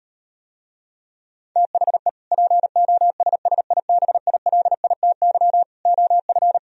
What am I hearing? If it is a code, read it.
Morse code, heard as a keyed tone, about 38 wpm, 700 Hz. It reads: THE POSSIBILITY OF